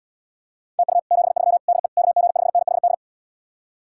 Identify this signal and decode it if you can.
Morse code, heard as a keyed tone, about 80 wpm, 700 Hz. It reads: TU 73 DE KG5NHD